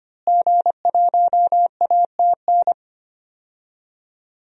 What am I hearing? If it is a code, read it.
Morse code, heard as a keyed tone, about 25 wpm, 700 Hz. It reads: G1ATN
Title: DXCC-100 Call Signs - 5 Characters - Repeated 3x - 25wpm